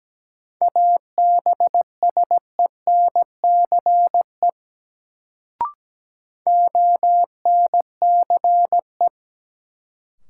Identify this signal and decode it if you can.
Morse code, heard as a keyed tone, about 17 wpm, 700 Hz. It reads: ABSENCE ONCE
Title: Sets of 4 Words - New Word Drill - 17wpm